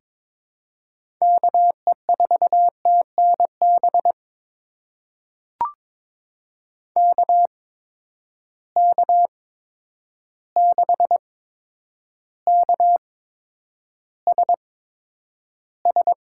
Morse code, heard as a keyed tone, about 22 wpm, spaced as for 9 wpm, 700 Hz. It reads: KE4TNB KK6KSS
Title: USA Call Signs - 6 Characters - 8x Character Spacing - 22wpm